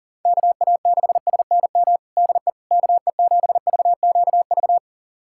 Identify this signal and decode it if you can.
Morse code, heard as a keyed tone, about 37 wpm, 690 Hz. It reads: KA6SNM DE KE7VQV
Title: USA Callsign DE Callsign - 6 Characters - 40wpm